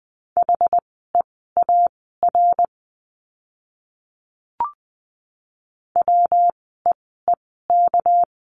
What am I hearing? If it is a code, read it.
Morse code, heard as a keyed tone, about 20 wpm, spaced as for 15 wpm, 700 Hz. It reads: HEAR WEEK
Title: 206 Lesson 33f - Letter j - Words 20wpm